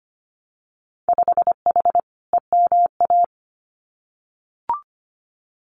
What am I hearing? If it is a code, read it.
Morse code, heard as a keyed tone, about 25 wpm, 690 Hz. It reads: 5H EMA
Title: Winter Field Day Exchanges Encore 25wpm